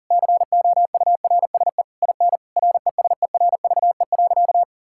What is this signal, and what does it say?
Morse code, heard as a keyed tone, about 40 wpm, 700 Hz. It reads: COURSE IN RESERVE.
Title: Sentences from Top 1000 Words - Review Part 6 - 40wpm